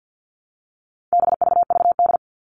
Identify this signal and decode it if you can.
Morse code, heard as a keyed tone, about 50 wpm, 700 Hz. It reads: BVVD
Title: All Cut Numbers - 4-Digit 50wpm